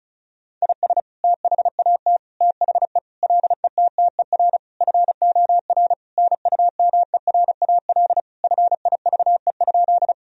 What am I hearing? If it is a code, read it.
Morse code, heard as a keyed tone, about 34 wpm, 700 Hz. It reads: IS THAT THE LETTER FOR NUMERAL FIVE?
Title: Sentences from Top 1000 Words - Review Part 2 - 35wpm